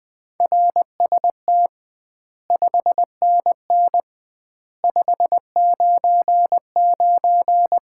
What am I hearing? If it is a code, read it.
Morse code, heard as a keyed tone, about 20 wpm, 700 Hz. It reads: RST 5NN 599
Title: Common QSO Phrases - 2x Word-Spacing - 20wpm